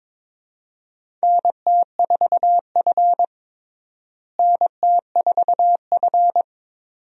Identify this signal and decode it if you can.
Morse code, heard as a keyed tone, about 21 wpm, 700 Hz. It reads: NT4F NT4F
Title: USA Call Signs - 4 Characters - Repeated 2x - 22wpm